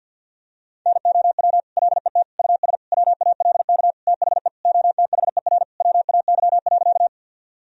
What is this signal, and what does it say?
Morse code, heard as a keyed tone, about 52 wpm, 700 Hz. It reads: NOW LET US WALK THE OTHER WAY.
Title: Sentences from Top 1000 Words - Review Part 1 - 50wpm